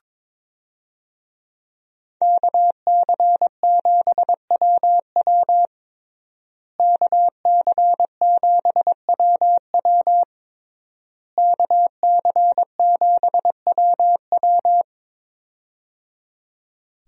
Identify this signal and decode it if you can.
Morse code, heard as a keyed tone, about 22 wpm, 700 Hz. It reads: KC7WW KC7WW KC7WW
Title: USA Call Signs - 5 Characters - Repeated 3x - 22wpm